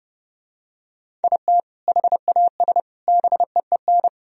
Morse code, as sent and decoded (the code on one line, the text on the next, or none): .. - / .... .- ... / -... . -.
IT HAS BEEN